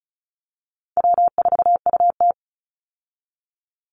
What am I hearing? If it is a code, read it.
Morse code, heard as a keyed tone, about 34 wpm, 700 Hz. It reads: W4UT
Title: Call Signs - 35wpm